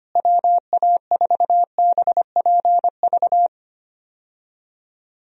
Morse code, heard as a keyed tone, about 25 wpm, 700 Hz. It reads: WA4BPV
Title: USA Call Signs - 6 Characters - Repeated 3x - 25wpm